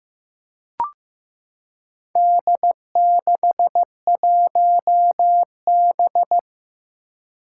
Morse code, heard as a keyed tone, about 15 wpm, 700 Hz. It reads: D61B
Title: Four Challenging Letter-Number Combinations 15wpm